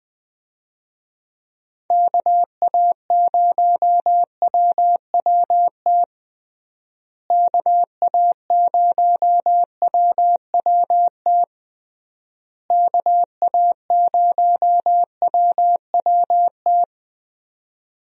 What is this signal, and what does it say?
Morse code, heard as a keyed tone, about 21 wpm, 700 Hz. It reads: KA0WWT KA0WWT KA0WWT
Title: USA Call Signs - 6 Characters - Repeated 3x - 20wpm